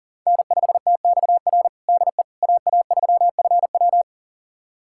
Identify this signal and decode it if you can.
Morse code, heard as a keyed tone, about 40 wpm, 700 Hz. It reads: N5TXR DE AA3FW
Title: USA Callsign DE Callsign - 5 Characters - 40wpm